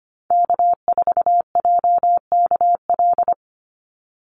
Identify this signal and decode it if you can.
Morse code, heard as a keyed tone, about 25 wpm, 700 Hz. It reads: K4JKL
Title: Call Signs Repeated 3x - 25wpm